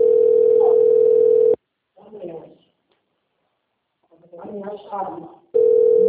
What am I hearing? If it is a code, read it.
Ringback tone (North America), 440 + 480 Hz, with speech.
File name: out- - - - - .12